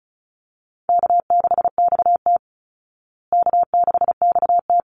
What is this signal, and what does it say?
Morse code, heard as a keyed tone, about 34 wpm, 700 Hz. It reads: K6XT K6XT
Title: Easy Call Signs Repeated 2x - 35wpm